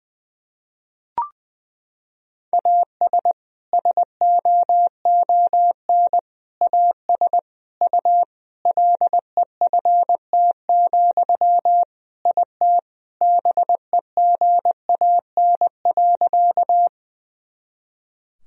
Morse code, heard as a keyed tone, about 20 wpm, 700 Hz. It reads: AS SOON AS U LEFT, IT BEGAN.
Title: Sentences from Top 400 Words and Abbreviations 20wpm